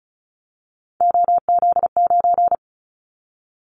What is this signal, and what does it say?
Morse code, heard as a keyed tone, about 34 wpm, 700 Hz. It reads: OZ9
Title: Callsign Prefixes for Top 100 DXCC - 35wpm